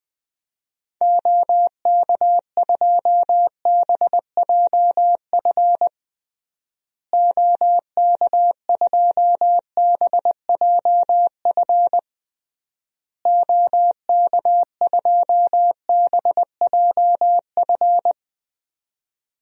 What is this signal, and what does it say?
Morse code, heard as a keyed tone, about 20 wpm, 700 Hz. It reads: OK2BJF OK2BJF OK2BJF
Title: DXCC-100 Call Signs - 6 Characters - Repeated 3x - 20wpm